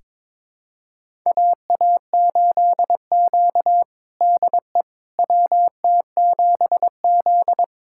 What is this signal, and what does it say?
Morse code, heard as a keyed tone, about 22 wpm, 700 Hz. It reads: AA8Q DE WT7Z
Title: USA Callsign DE Callsign - 4 Characters - 22wpm